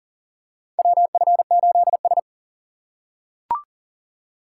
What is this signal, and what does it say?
Morse code, heard as a keyed tone, about 40 wpm, 700 Hz. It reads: WF8S
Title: USA Call Signs - 4 Characters - 4x Character Spacing - 40wpm